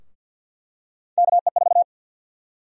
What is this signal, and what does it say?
Morse code, heard as a keyed tone, about 51 wpm, 700 Hz. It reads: KE4
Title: USA Call Sign Prefixes - Repeated 2x - 50wpm